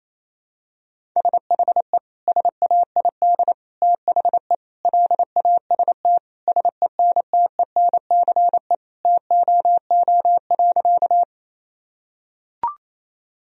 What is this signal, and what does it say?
Morse code, heard as a keyed tone, about 28 wpm, 700 Hz. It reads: SHE SAID THE LAST SENTENCE TOO.
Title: Sentences from Top 1000 Words - Review Part 1 - 28wpm